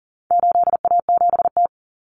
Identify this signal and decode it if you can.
Morse code, heard as a keyed tone, about 40 wpm, 700 Hz. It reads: 8A7T
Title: Common Cut Numbers - 4-Digit 40wpm